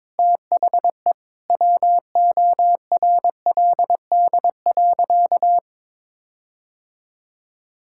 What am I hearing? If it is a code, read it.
Morse code, heard as a keyed tone, about 22 wpm, 700 Hz. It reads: THE WORLD.
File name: practice / 22wpm / Sentences with 7-Word Phrases - 22wpm